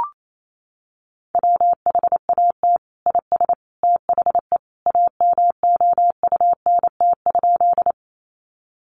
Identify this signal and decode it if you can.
Morse code, heard as a keyed tone, about 28 wpm, 700 Hz. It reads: WHAT IS THE AMOUNT?
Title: Sentences from Top 700 Words 28wpm